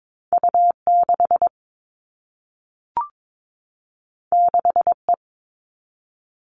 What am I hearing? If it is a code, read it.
Morse code, heard as a keyed tone, about 22 wpm, 700 Hz. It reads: U6 6E
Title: All Cut Numbers - 2-Digit 22wpm